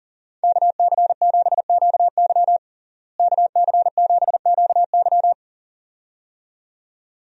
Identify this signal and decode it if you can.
Morse code, heard as a keyed tone, about 40 wpm, 700 Hz. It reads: KC7QY KC7QY
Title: USA Call Signs - 5 Characters - Repeated 2x - 40wpm